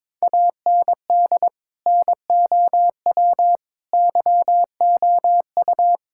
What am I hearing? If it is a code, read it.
Morse code, heard as a keyed tone, about 22 wpm, 700 Hz. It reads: AND NOW YOU
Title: Sentences from Top 1000 Words - Review Part 6 - 22wpm